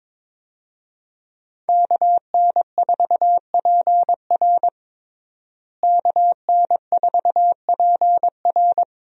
Morse code, heard as a keyed tone, about 21 wpm, 700 Hz. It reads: KN4PR KN4PR
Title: USA Call Signs - 5 Characters - Repeated 2x - 22wpm